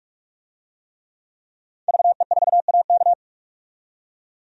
Morse code, heard as a keyed tone, about 44 wpm, 690 Hz. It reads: VE4AK
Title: DXCC-100 Call Signs - 5 Characters - 45wpm